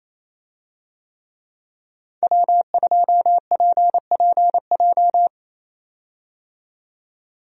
Morse code, heard as a keyed tone, about 28 wpm, 700 Hz. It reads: W2PPJ
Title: USA Call Signs - 5 Characters - 28wpm